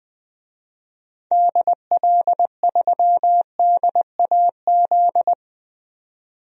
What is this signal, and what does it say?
Morse code, heard as a keyed tone, about 20 wpm, 700 Hz. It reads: DL3DAZ
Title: DXCC-100 Call Signs - 6 Characters - 20wpm